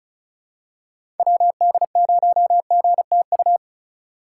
Morse code, heard as a keyed tone, about 36 wpm, 700 Hz. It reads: WD0GTU
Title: USA Call Signs - 6 Characters - 3x Character Spacing - 35wpm